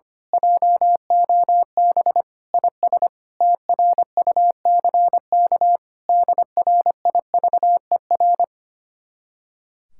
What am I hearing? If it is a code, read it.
Morse code, heard as a keyed tone, about 24 wpm, 700 Hz. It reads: JOB IS TRUCK DRIVER
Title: Speed-Racing Common QSO Phrases - 25wpm